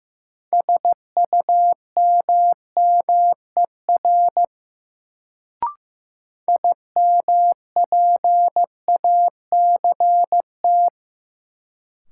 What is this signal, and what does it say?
Morse code, heard as a keyed tone, about 15 wpm, 700 Hz. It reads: SUMMER IMPACT
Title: Sentences from Top 1000 Words - Review Part 6 - New Word Drill - 15wpm